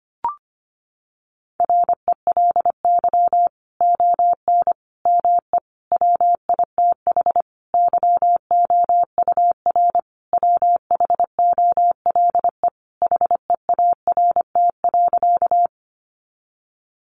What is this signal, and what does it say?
Morse code, heard as a keyed tone, about 25 wpm, 700 Hz. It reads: RELY ON ME WITH YOUR WHOLE HEART.
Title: Sentences from Top 1200 Words 25wpm